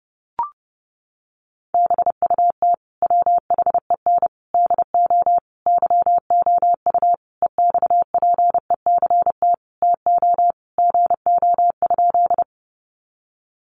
Morse code, heard as a keyed tone, about 30 wpm, 700 Hz. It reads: BUT WHEN DO YOU EXPECT TO GO?
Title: Sentences from Top 1100 Words 30wpm